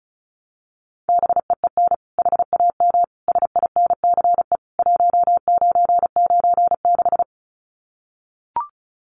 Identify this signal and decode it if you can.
Morse code, heard as a keyed tone, about 34 wpm, 700 Hz. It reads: BEEN HAM SINCE 1996
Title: Common QSO Phrases 35wpm